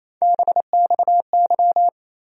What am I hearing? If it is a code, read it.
Morse code, heard as a keyed tone, about 28 wpm, 700 Hz. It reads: BXY
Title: USA Call Sign Postfixes - 3 Letters 28wpm